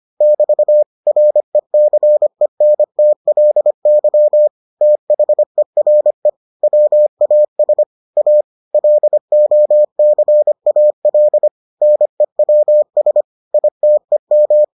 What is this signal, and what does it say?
Morse code, heard as a keyed tone, about 25 wpm, 600 Hz. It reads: <BT> RECENTLY THERE WAS A LOCAL NEWS ITEM